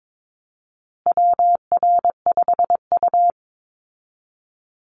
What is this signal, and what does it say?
Morse code, heard as a keyed tone, about 22 wpm, 700 Hz. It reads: WR5U
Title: Call Signs Repeated 2x - 22wpm